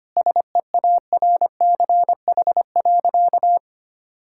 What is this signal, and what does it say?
Morse code, heard as a keyed tone, about 25 wpm, 700 Hz. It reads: SEARCH.
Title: Sentences from Top 1000 Words - Review Part 5 - 25wpm